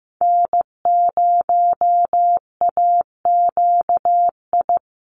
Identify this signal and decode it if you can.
Morse code, heard as a keyed tone, about 15 wpm, 700 Hz. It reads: N0AQI
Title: USA Call Signs 15wpm